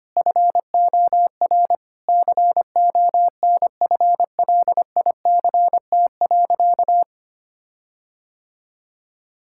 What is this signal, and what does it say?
Morse code, heard as a keyed tone, about 25 wpm, 700 Hz. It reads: FOR CONFLICT.
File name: practice / 25wpm / All-Age Appropriate Jokes - 25wpm